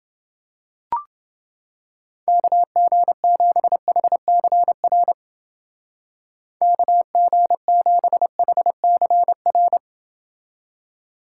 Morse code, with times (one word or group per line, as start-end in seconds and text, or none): 2.3-5.1 KG7HCR
6.6-9.8 KG7HCR